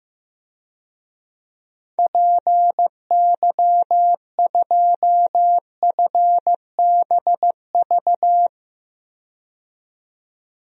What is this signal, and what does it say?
Morse code, heard as a keyed tone, about 15 wpm, 700 Hz. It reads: PY2FBV
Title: DXCC-100 Call Signs - 6 Characters - 15wpm